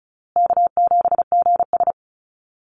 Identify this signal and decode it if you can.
Morse code, heard as a keyed tone, about 34 wpm, 700 Hz. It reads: K7GS
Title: Easy Call Signs Repeated 3x - 35wpm